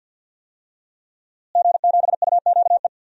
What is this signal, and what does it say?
Morse code, heard as a keyed tone, about 51 wpm, 700 Hz. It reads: G7UQE
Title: DXCC-100 Call Signs - 5 Characters - Repeated 2x - 50wpm